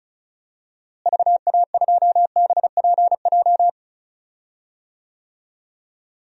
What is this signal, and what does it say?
Morse code, heard as a keyed tone, about 34 wpm, 700 Hz. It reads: VA2BPJ
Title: DXCC-100 Call Signs - 6 Characters - Repeated 3x - 35wpm